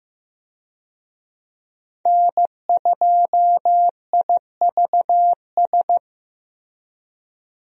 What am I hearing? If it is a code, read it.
Morse code, heard as a keyed tone, about 15 wpm, 700 Hz. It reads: N2IVS
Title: USA Call Signs - 5 Characters - Repeated 2x - 15wpm